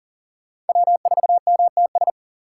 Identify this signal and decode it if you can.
Morse code, heard as a keyed tone, about 53 wpm, spaced as for 47 wpm, 700 Hz. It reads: W4MTS
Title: Speed Racing - USA 1x3 Call Signs 40wpm